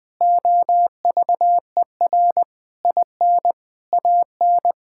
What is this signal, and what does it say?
Morse code, heard as a keyed tone, about 20 wpm, 700 Hz. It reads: OVER IN AN
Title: Sentences from Top 1000 Words - Review Part 5 - 20wpm